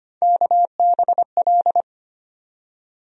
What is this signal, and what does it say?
Morse code, heard as a keyed tone, about 25 wpm, 700 Hz. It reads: KBL
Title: USA Call Sign Postfixes - 3 Letters 25wpm